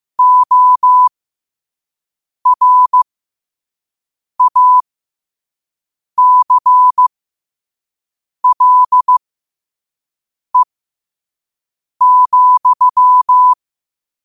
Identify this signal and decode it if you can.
Morse code, heard as a keyed tone, about 15 wpm, spaced as for 5 wpm, 1000 Hz. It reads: ORACLE,
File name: qotdfiles / 5wpm